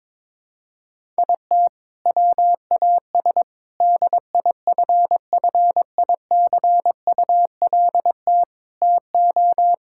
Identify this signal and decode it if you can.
Morse code, heard as a keyed tone, about 22 wpm, 700 Hz. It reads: IT WAS DIFFICULT TO
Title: Sentences from Top 1000 Words - Review Part 5 - 22wpm